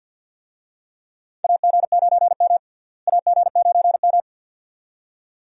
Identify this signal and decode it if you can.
Morse code, heard as a keyed tone, about 52 wpm, 700 Hz. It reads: AG9M AG9M
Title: USA Call Signs - 4 Characters - Repeated 2x - 50wpm